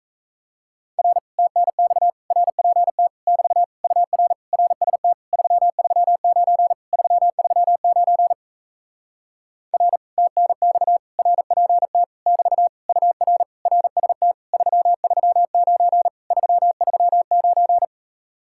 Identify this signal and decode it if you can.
Morse code, heard as a keyed tone, about 37 wpm, 700 Hz. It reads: R TNX RPT <BT> UR RST 339 339 R TNX RPT <BT> UR RST 339 339